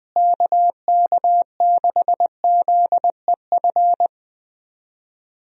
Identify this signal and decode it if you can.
Morse code, heard as a keyed tone, about 20 wpm, 700 Hz. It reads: KK6ZEF
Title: USA Call Signs - 6 Characters - Repeated 3x - 20wpm